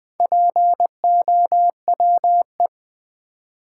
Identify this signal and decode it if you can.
Morse code, heard as a keyed tone, about 20 wpm, 700 Hz. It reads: POWE
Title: Build A Word - 1 Letter At A Time - 5-Letter Words - 20wpm